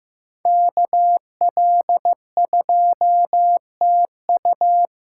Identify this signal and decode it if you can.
Morse code, heard as a keyed tone, about 15 wpm, 700 Hz. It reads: KL2TU
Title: USA Callsign DE Callsign - 5 Characters - 15wpm